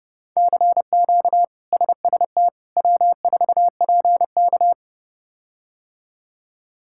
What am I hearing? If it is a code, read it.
Morse code, heard as a keyed tone, about 30 wpm, 700 Hz. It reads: CQ SST W4PK
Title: K1USN SST QSO Practice - 30wpm